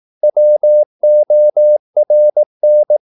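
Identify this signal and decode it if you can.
Morse code, heard as a keyed tone, about 18 wpm, 600 Hz. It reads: WORN
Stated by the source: Morse practice at 18 wpm, English text, from CW med Gnister.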